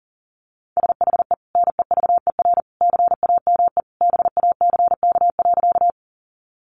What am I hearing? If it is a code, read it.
Morse code, heard as a keyed tone, about 40 wpm, 700 Hz. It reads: SHE NEVER CAME BACK.
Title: Sentences from Top 300 Words 40wpm